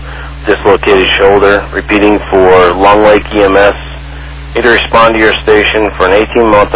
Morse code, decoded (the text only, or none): none